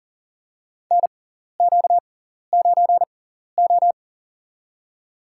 Morse code, heard as a keyed tone, about 40 wpm, spaced as for 14 wpm, 700 Hz. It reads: NQ9O
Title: USA Call Signs - 4 Characters - 6x Character Spacing - 40wpm